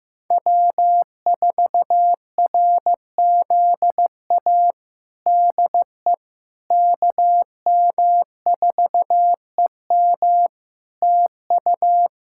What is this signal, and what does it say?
Morse code, heard as a keyed tone, about 15 wpm, 700 Hz. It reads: W4RZA DE KM4EM TU
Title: Practice QSO - Exchange 4 - Ending - 15wpm